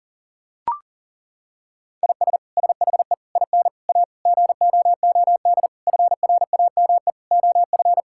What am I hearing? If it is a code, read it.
Morse code, heard as a keyed tone, about 37 wpm, 700 Hz. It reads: IS SHE IN A GOOD FRAME OF
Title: Sentences from Top 1000 Words - Review Part 6 - 40wpm